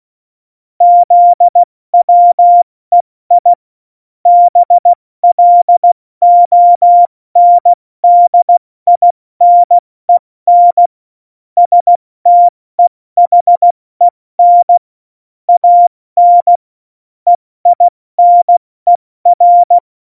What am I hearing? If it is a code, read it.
Morse code, heard as a keyed tone, about 16 wpm, spaced as for 14 wpm, 700 Hz. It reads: ZWEI BLONDINEN STEHEN AN EINER
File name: Morsekurs / Witze / Witz 18 Juli 17, Tempo 70 F80